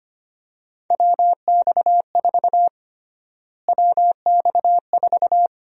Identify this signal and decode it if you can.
Morse code, heard as a keyed tone, about 25 wpm, 700 Hz. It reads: WX4 WX4
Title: USA Call Sign Prefixes - Repeated 2x - 25wpm